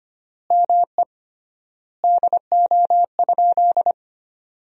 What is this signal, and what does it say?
Morse code, heard as a keyed tone, about 25 wpm, 700 Hz. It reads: ME DO?
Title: 3x Word-Spacing - Sentences from Top 200 Words - 25wpm